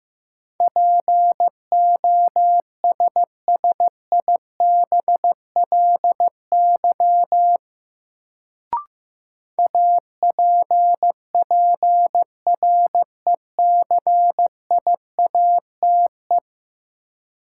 Morse code, heard as a keyed tone, about 15 wpm, 700 Hz. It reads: POSSIBLY APPRECIATE
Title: Sentences from Top 1300 Words - New Word Drill Alt - 15wpm